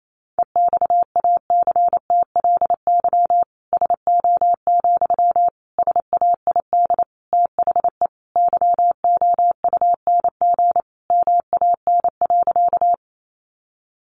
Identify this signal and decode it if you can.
Morse code, heard as a keyed tone, about 28 wpm, 700 Hz. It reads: EXACTLY SO, SAID THE YOUNG MAN.
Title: Sentences from Top 900 Words 28wpm